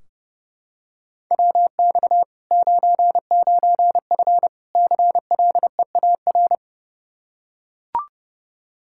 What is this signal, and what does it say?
Morse code, heard as a keyed tone, about 29 wpm, 700 Hz. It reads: WX 99F CLEAR
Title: Practice QSO - Exchange 2 - Info - 30wpm